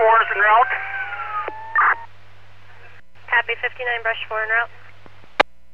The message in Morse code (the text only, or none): none